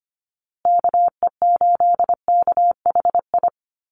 Morse code, heard as a keyed tone, about 25 wpm, 700 Hz. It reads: KE8KHI
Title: USA Call Signs Repeated 2x 25wpm